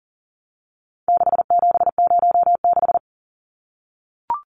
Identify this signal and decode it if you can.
Morse code, heard as a keyed tone, about 40 wpm, 700 Hz. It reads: 6706
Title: Numbers - 4-Digit 40wpm